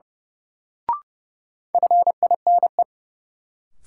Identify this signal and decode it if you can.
Morse code, heard as a keyed tone, about 29 wpm, 700 Hz. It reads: FINE
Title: Sentences from Top 500 Words - New Word Drill - 30wpm